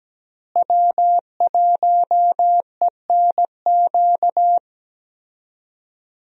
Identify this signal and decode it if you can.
Morse code, heard as a keyed tone, about 17 wpm, 700 Hz. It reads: W1ENQ
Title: USA Call Signs - 5 Characters - Repeated 3x - 17wpm